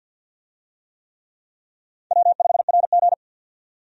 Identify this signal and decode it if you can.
Morse code, heard as a keyed tone, about 53 wpm, 700 Hz. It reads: W5RG
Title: USA Call Signs - 4 Characters - 50wpm